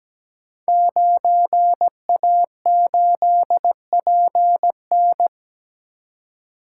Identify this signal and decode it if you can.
Morse code, heard as a keyed tone, about 17 wpm, 700 Hz. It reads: 9A8PN
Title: Code Groups - 5 Letter:Numbers - 17wpm